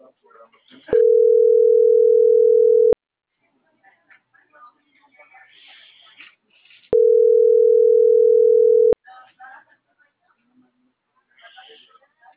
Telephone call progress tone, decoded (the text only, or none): ringback tone